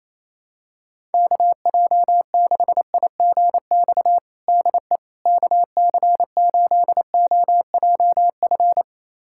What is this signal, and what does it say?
Morse code, heard as a keyed tone, about 28 wpm, 700 Hz. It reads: KJ6IGX DE KC8OJF
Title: USA Callsign DE Callsign - 6 Characters - 28wpm